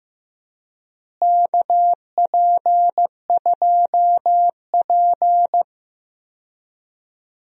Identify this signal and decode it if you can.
Morse code, heard as a keyed tone, about 15 wpm, 700 Hz. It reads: KP2P
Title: DXCC-100 Call Signs - 4 Characters - Repeated 2x - 15wpm